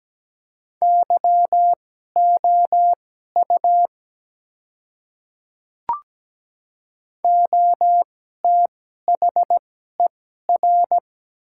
Morse code, heard as a keyed tone, about 17 wpm, spaced as for 12 wpm, 700 Hz. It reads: YOU OTHER
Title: 2x Character Spacing - Top 100 Words - 17wpm